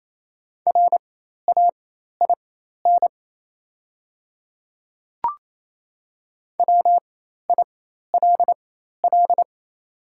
Morse code, heard as a keyed tone, about 28 wpm, spaced as for 13 wpm, 700 Hz. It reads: RAIN WILL
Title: 4x Character Spacing - Top 300 Words - 28wpm